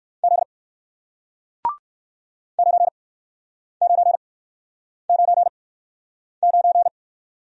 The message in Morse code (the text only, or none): Z 9999